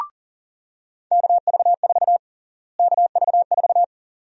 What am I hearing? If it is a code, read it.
Morse code, heard as a keyed tone, about 40 wpm, 700 Hz. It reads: KV4 KV4
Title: USA Call Sign Prefixes - Repeated 2x - 40wpm